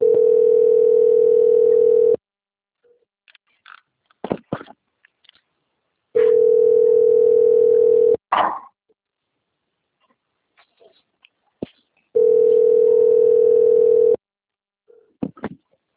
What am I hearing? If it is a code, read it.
Ringback tone (North America), 440 + 480 Hz.